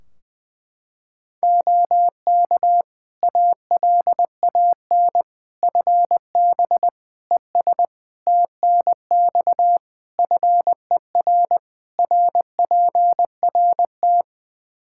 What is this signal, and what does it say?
Morse code, heard as a keyed tone, about 20 wpm, 700 Hz. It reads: OK ALAN FB ES TNX FER RPRT
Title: Practice QSO - Exchange 2 - Info - 20wpm